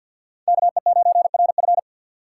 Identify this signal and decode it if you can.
Morse code, heard as a keyed tone, about 52 wpm, 700 Hz. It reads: KE9RF
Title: USA Call Signs - 5 Characters - 3x Character Spacing - 50wpm